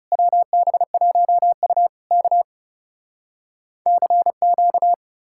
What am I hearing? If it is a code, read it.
Morse code, heard as a keyed tone, about 40 wpm, 700 Hz. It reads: WB1U K CQ